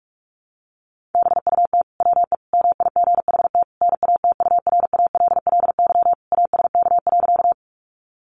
Morse code, heard as a keyed tone, about 44 wpm, 700 Hz. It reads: BUT WE MIGHT NATURALLY ASK.
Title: Sentences from Top 1300 Words 45wpm